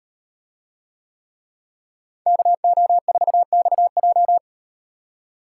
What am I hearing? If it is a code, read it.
Morse code, heard as a keyed tone, about 38 wpm, 700 Hz. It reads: KO4XJ